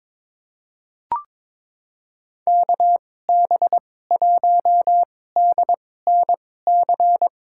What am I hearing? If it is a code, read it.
Morse code, heard as a keyed tone, about 22 wpm, spaced as for 16 wpm, 700 Hz. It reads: KB1DNC
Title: USA Call Signs - 6 Characters - 2x Character Spacing - 22wpm